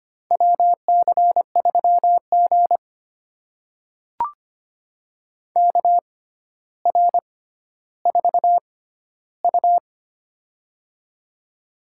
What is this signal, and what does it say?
Morse code, heard as a keyed tone, about 25 wpm, spaced as for 13 wpm, 700 Hz. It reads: WC3G KR4U
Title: USA Call Signs - 4 Characters - 6x Character Spacing - 25wpm